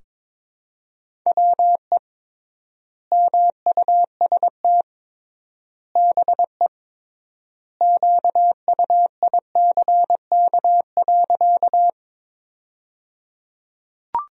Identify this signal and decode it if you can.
Morse code, heard as a keyed tone, about 22 wpm, 700 Hz. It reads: WE MUST BE QUICK.
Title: 3x Word-Spacing - Sentences from Top 500 Words - 22wpm